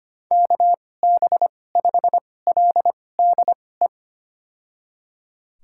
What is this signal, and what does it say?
Morse code, heard as a keyed tone, about 25 wpm, spaced as for 18 wpm, 690 Hz. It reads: KB5LDE
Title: USA Call Signs - 6 Characters - 2x Character Spacing - 25wpm